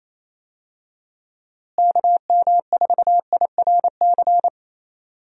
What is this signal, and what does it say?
Morse code, heard as a keyed tone, about 28 wpm, 700 Hz. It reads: KM4IRC
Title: USA Call Signs - 6 Characters - Repeated 3x - 28wpm